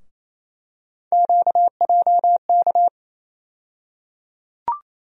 Morse code, heard as a keyed tone, about 28 wpm, 700 Hz. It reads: QJK
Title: USA Call Sign Postfixes - 3 Letters 28wpm